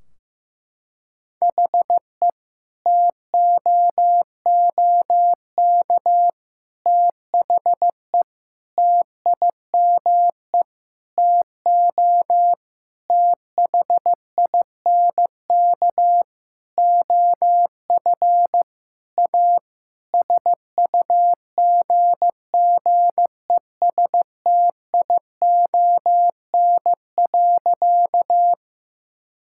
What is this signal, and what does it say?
Morse code, heard as a keyed tone, about 15 wpm, 700 Hz. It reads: HE TOOK THE TIME TO THINK OF A SUGGESTION.
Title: Sentences from Top 1600 Words 15wpm